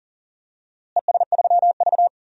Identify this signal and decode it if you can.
Morse code, heard as a keyed tone, about 40 wpm, 700 Hz. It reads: ES3V
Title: DXCC-100 Call Signs - 4 Characters - Repeated 2x - 40wpm